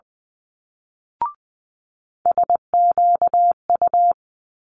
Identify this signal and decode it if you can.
Morse code, heard as a keyed tone, about 20 wpm, 700 Hz. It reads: SQU
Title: Multi-letter Phonemes Quick Fire 20wpm